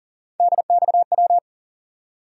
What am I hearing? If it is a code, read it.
Morse code, heard as a keyed tone, about 40 wpm, 700 Hz. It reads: DXW
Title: USA Call Sign Postfixes - 3 Letters 40wpm